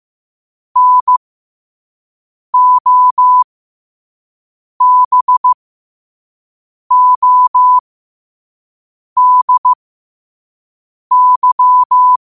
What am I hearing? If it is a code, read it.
Morse code, heard as a keyed tone, about 15 wpm, spaced as for 5 wpm, 1000 Hz. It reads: NOBODY
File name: qotdfiles / 5wpm